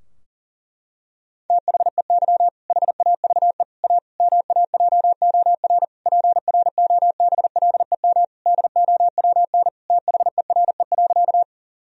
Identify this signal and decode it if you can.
Morse code, heard as a keyed tone, about 40 wpm, 700 Hz. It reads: THEY HAVE A MAJOR PROBLEM DOWN THERE.
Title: Sentences from Top 1000 Words - Review Part 4 - 40wpm